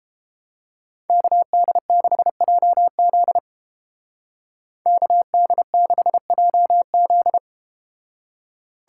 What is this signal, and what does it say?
Morse code, heard as a keyed tone, about 32 wpm, 700 Hz. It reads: KD6JZ KD6JZ